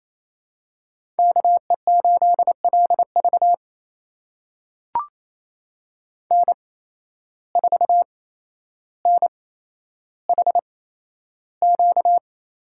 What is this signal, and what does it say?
Morse code, heard as a keyed tone, about 28 wpm, spaced as for 12 wpm, 700 Hz. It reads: KE8LV N4NHQ
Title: USA Call Signs - 5 Characters - 8x Character Spacing - 28wpm